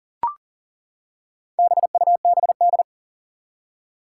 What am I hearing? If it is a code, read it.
Morse code, heard as a keyed tone, about 40 wpm, 700 Hz. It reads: BUBD
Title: Four Challenging Letter-Number Combinations 40wpm